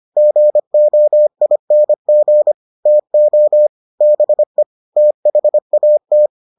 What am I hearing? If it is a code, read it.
Morse code, heard as a keyed tone, about 25 wpm, 600 Hz. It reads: GOING TO BE THAT